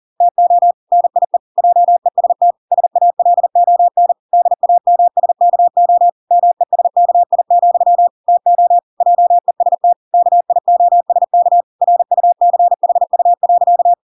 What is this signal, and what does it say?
Morse code, heard as a keyed tone, about 40 wpm, 700 Hz. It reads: TO NIE JEST SALON DAMSKO MESKI, TO JEST KIOSK RUCHU.